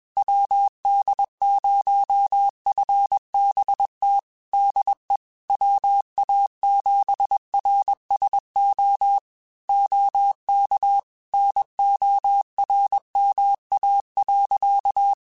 Morse code, heard as a keyed tone, about 21 wpm, 800 Hz. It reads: WD0FBT DE WA7RSO OK NORMA.
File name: QSOs / QSOs-mp3s / WD0FBT 1@20